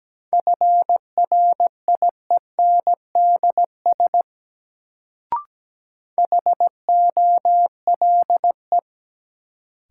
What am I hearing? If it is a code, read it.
Morse code, heard as a keyed tone, about 17 wpm, 700 Hz. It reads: FRIENDS HOLE
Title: Sentences from Top 1600 Words - New Word Drill Alt - 17wpm